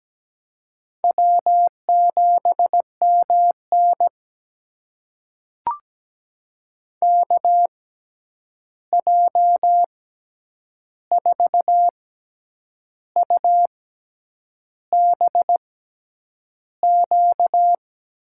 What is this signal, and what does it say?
Morse code, heard as a keyed tone, about 17 wpm, spaced as for 8 wpm, 700 Hz. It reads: W7MN KJ4UBQ
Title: USA Call Signs - 6x Character Spacing - 17wpm